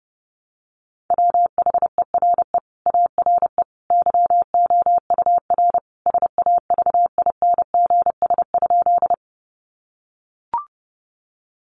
Morse code, heard as a keyed tone, about 30 wpm, 700 Hz. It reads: WHERE ARE YOUR SAVINGS?
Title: Sentences from Top 1200 Words 30wpm